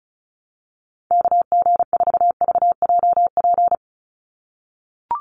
Morse code, heard as a keyed tone, about 34 wpm, 700 Hz. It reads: KG4VJP
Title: USA Call Signs 35wpm